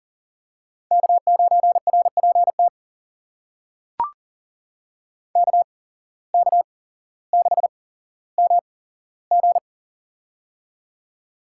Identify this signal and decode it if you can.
Morse code, heard as a keyed tone, about 39 wpm, spaced as for 17 wpm, 700 Hz. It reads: K9RPT KK6MG
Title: USA Call Signs - 5 Characters - 8x Character Spacing - 40wpm